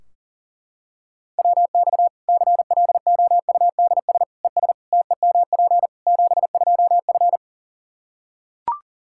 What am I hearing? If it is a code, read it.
Morse code, heard as a keyed tone, about 40 wpm, 700 Hz. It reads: WX CLOUDS ES TEMP 72F
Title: Practice QSO - Exchange 2 - Info - 40wpm